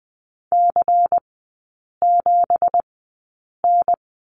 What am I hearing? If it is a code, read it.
Morse code, heard as a keyed tone, about 20 wpm, spaced as for 8 wpm, 700 Hz. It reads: C7N
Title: 172 Lesson 28e - Number 9 - Three Characters 20wpm